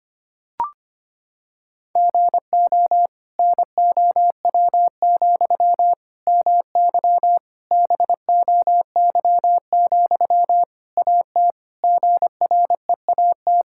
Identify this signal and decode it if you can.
Morse code, heard as a keyed tone, about 25 wpm, 700 Hz. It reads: GO NOW, MY BOY, AT GREAT
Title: Sentences from Top 1000 Words - Review Part 5 - 25wpm